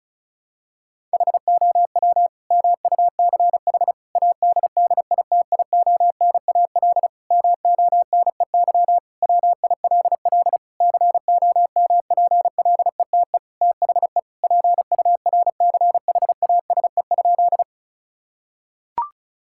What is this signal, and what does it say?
Morse code, heard as a keyed tone, about 36 wpm, 700 Hz. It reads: HOW MUCH ADDITIONAL MONEY WILL COMPLETE THE PURCHASE?
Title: Sentences from Top 1000 Words - Review Part 5 - 35wpm